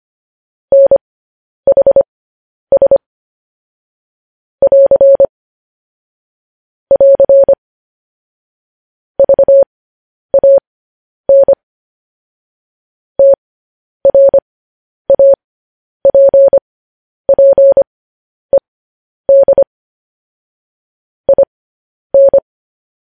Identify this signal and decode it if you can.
Morse code, heard as a keyed tone, about 25 wpm, spaced as for 10 wpm, 550 Hz. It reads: NHS <AR> <AR> VAN TRAPPED IN
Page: news headlines in morse code at 10 wpm